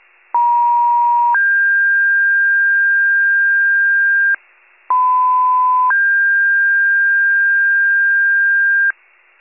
Two-tone page, heard as a one-second tone, then a three-second tone, about 950 then 1600 Hz, about 1010 then 1600 Hz, with noise. It is heard from 0.3 to 8.9 s.
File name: 83 RESCUE DUTY